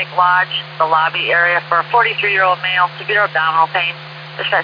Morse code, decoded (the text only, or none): none